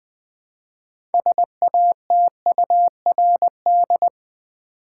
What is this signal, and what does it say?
Morse code, heard as a keyed tone, about 20 wpm, 700 Hz. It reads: SATURD
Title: Build A Word - 2 Letters At A Time - 8-Letter Words - 20wpm